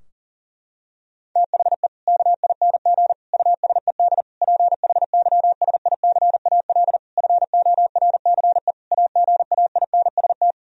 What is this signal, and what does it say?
Morse code, heard as a keyed tone, about 40 wpm, 700 Hz. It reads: THE KING USED PHYSICAL FORCE AGAINST
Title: Sentences from Top 1000 Words - Review Part 4 - 40wpm